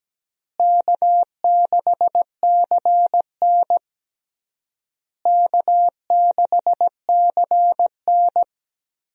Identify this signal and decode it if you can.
Morse code, heard as a keyed tone, about 17 wpm, 700 Hz. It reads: K6CN K6CN
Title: USA Call Signs - 4 Characters - Repeated 3x - 17wpm